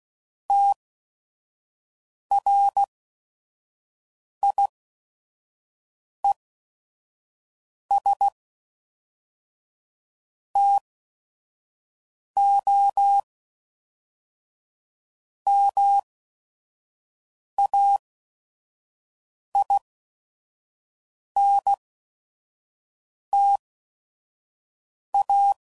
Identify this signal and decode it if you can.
Morse code, heard as a keyed tone, about 16 wpm, 800 Hz. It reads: T R I E S T O M A I N T A